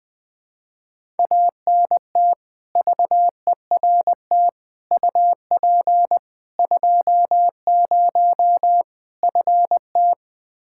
Morse code, heard as a keyed tone, about 21 wpm, 700 Hz. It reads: ANT VERT UP 20 FT
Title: K1USN SST QSO Practice 20wpm